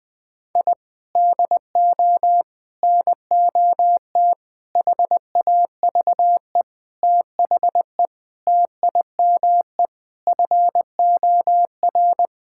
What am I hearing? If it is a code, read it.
Morse code, heard as a keyed tone, about 20 wpm, 700 Hz. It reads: I DO NOT HAVE THE TIME FOR
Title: Sentences with 5-Word Phrases - Spoken Beginning - 20wpm